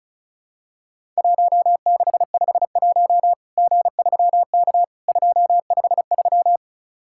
Morse code, heard as a keyed tone, about 34 wpm, 700 Hz. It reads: 1651 G3K 253
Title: Worked All Europe DX Contest - QTC Exchange - 35wpm